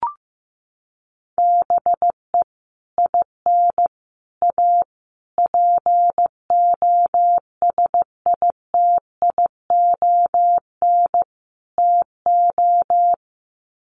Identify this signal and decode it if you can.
Morse code, heard as a keyed tone, about 15 wpm, 700 Hz. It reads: BE IN A POSITION TO
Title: Sets of 5 Words Encore2 15wpm